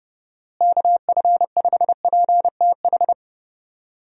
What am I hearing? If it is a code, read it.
Morse code, heard as a keyed tone, about 32 wpm, 700 Hz. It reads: KF5PTH
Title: USA Call Signs - 6 Characters - Repeated 2x - 30wpm